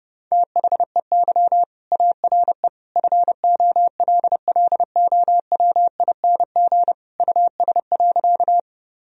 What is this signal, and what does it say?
Morse code, heard as a keyed tone, about 30 wpm, 700 Hz. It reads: THEY ARE FOLLOWING US.
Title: Sentences from Top 1000 Words - Review Part 4 - 30wpm